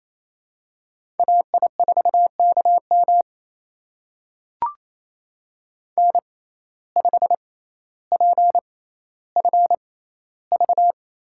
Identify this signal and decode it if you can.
Morse code, heard as a keyed tone, about 27 wpm, spaced as for 14 wpm, 700 Hz. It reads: AI4KM N5PFV
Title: USA Call Signs - 5 Characters - 6x Character Spacing - 28wpm